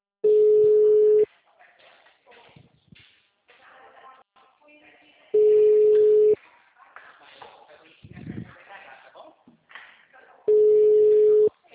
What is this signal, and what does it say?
Ringback tone (Europe, 425 Hz), with speech.